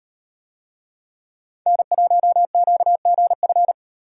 Morse code, heard as a keyed tone, about 36 wpm, 700 Hz. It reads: N1QGF